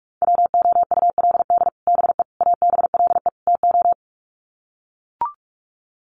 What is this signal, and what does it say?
Morse code, heard as a keyed tone, about 44 wpm, 700 Hz. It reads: WOULD BE ABLE TO
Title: Sets of 4 Words 45wpm